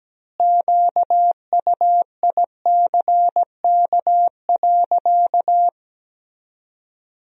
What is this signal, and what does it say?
Morse code, heard as a keyed tone, about 17 wpm, 700 Hz. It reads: QUICK.
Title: 2x Word-Spacing - Sentences from Top 500 Words - 17wpm